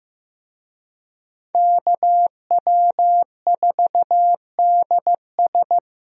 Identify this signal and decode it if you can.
Morse code, heard as a keyed tone, about 15 wpm, 700 Hz. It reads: KW4DS
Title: USA Call Signs - 5 Characters - Repeated 2x - 15wpm